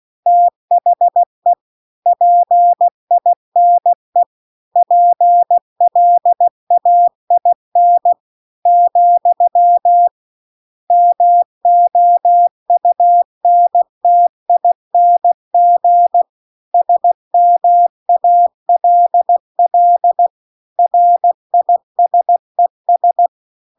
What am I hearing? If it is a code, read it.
Morse code, heard as a keyed tone, about 16 wpm, 700 Hz. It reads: THE PINE PLAIN , MOUNTING SMALL RISES